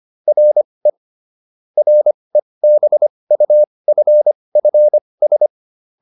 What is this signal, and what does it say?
Morse code, heard as a keyed tone, about 25 wpm, spaced as for 20 wpm, 600 Hz. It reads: RE REBUFFS
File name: RE7 20